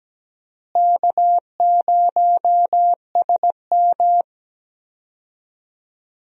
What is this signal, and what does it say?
Morse code, heard as a keyed tone, about 17 wpm, 700 Hz. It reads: K0SM
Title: USA Call Signs - 4 Characters - Repeated 2x - 17wpm